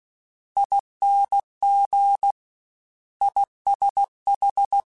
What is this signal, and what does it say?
Morse code, heard as a keyed tone, about 16 wpm, 800 Hz. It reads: ING ISH
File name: Misc / Misc-mp3s / Jumps @15